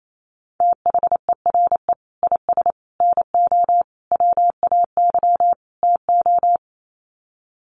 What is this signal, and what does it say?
Morse code, heard as a keyed tone, about 28 wpm, 700 Hz. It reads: THERE IS NO WAY TO
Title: Sets of 5 Words Encore 28wpm